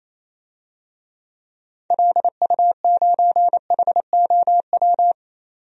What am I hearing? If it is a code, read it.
Morse code, heard as a keyed tone, about 28 wpm, 700 Hz. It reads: LU9HOW
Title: DXCC-100 Call Signs - 6 Characters - 28wpm